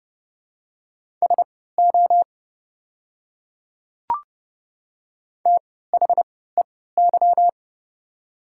Morse code, heard as a keyed tone, about 32 wpm, spaced as for 17 wpm, 700 Hz. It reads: SO THEY
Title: 3x Character Spacing - Top 100 Words - 30wpm